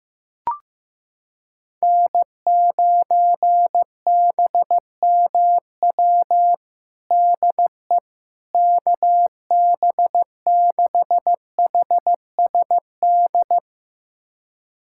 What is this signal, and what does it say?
Morse code, heard as a keyed tone, about 15 wpm, 700 Hz. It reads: N9BMW DE KB6HSD
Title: USA Callsign DE Callsign 15wpm